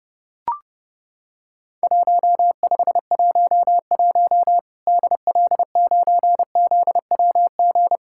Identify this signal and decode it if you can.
Morse code, heard as a keyed tone, about 30 wpm, 700 Hz. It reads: 1511 DL9ZWG
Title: Worked All Europe DX Contest - QTC Exchange - 30wpm